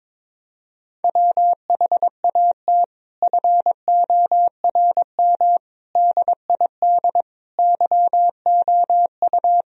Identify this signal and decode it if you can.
Morse code, heard as a keyed tone, about 22 wpm, 700 Hz. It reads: WHAT FORM DID YOU
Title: Sentences from Top 1000 Words - Review Part 1 - 22wpm